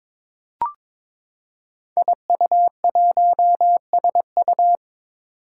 Morse code, heard as a keyed tone, about 22 wpm, 700 Hz. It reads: IU1SU
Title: Five Challenging Letter-Number Combinations 22wpm